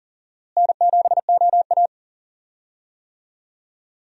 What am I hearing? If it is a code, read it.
Morse code, heard as a keyed tone, about 40 wpm, 700 Hz. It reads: N7OA
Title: USA Call Signs - 4 Characters - Repeated 3x - 40wpm